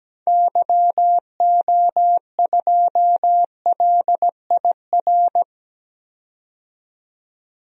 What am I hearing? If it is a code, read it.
Morse code, heard as a keyed tone, about 17 wpm, 700 Hz. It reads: YO2LIR
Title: DXCC-100 Call Signs - 6 Characters - 17wpm